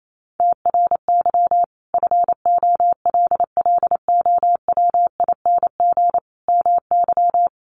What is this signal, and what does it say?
Morse code, heard as a keyed tone, about 28 wpm, 700 Hz. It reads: TRY FOLLOWING MY
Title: Sentences from Top 1100 Words 28wpm